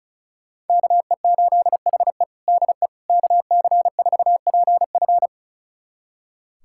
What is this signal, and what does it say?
Morse code, heard as a keyed tone, about 36 wpm, 700 Hz. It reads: KE8HE DE KC4PF
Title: USA Callsign DE Callsign - 5 Characters - 35wpm